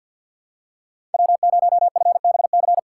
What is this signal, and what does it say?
Morse code, heard as a keyed tone, about 53 wpm, 700 Hz. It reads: W0FBC
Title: USA Call Signs - 5 Characters - 3x Character Spacing - 50wpm